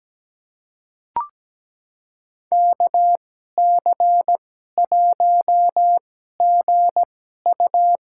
Morse code, heard as a keyed tone, about 17 wpm, spaced as for 12 wpm, 700 Hz. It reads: KC1GU
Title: USA Call Signs - 2x Character Spacing - 17wpm